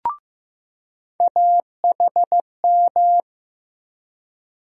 Morse code, heard as a keyed tone, about 15 wpm, 700 Hz. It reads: AHM
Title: USA Call Sign Postfixes - 3 Letters 15wpm